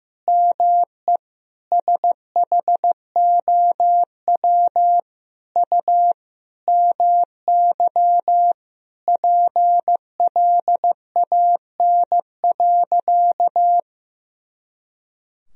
Morse code, heard as a keyed tone, about 15 wpm, 700 Hz. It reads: ME SHOW U MY PLAN.
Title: Sentences from Top 500 Words and Abbreviations 15wpm